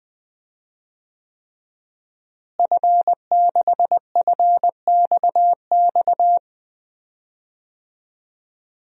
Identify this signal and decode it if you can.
Morse code, heard as a keyed tone, about 20 wpm, 700 Hz. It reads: F6FXX
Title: DXCC-100 Call Signs - 5 Characters - Repeated 3x - 20wpm